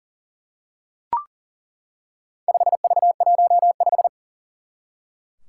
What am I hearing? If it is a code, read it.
Morse code, heard as a keyed tone, about 40 wpm, 700 Hz. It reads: 5V15
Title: Four Challenging Letter-Number Combinations 40wpm